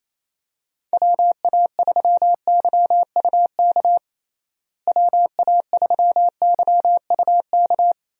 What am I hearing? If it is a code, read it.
Morse code, heard as a keyed tone, about 27 wpm, 700 Hz. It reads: WA3YUK WA3YUK
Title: USA Call Signs - 6 Characters - Repeated 2x - 28wpm